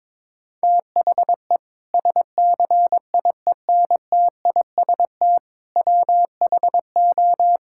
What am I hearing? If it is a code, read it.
Morse code, heard as a keyed tone, about 22 wpm, 700 Hz. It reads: THE SCIENTIST WHO
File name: practice / 22wpm / Sentences with 8-Word Phrases - 22wpm